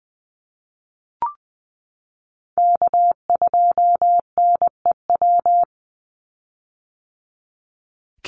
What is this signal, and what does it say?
Morse code, heard as a keyed tone, about 20 wpm, 700 Hz. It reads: K2NEW
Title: USA Call Signs 20wpm